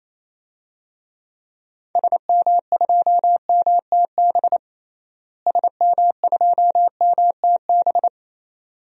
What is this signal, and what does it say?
Morse code, heard as a keyed tone, about 28 wpm, 700 Hz. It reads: SM2MTB SM2MTB
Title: DXCC-100 Call Signs - 6 Characters - Repeated 2x - 28wpm